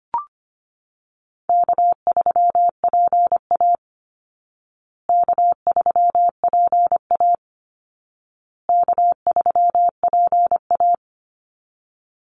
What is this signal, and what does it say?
Morse code, heard as a keyed tone, about 25 wpm, 700 Hz. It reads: K3PA K3PA K3PA
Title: Easy Call Signs Repeated 3x - 25wpm